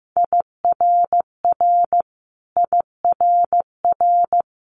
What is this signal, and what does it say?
Morse code, heard as a keyed tone, about 15 wpm, 700 Hz. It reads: IRR IRR
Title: Multi-letter Phonemes Repeated 3x 15wpm